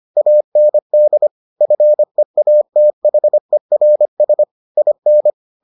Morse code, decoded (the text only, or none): AND FEATHERS IN